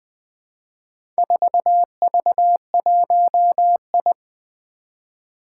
Morse code, heard as a keyed tone, about 20 wpm, 700 Hz. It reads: 4V1I
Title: Four Challenging Letter-Number Combinations 20wpm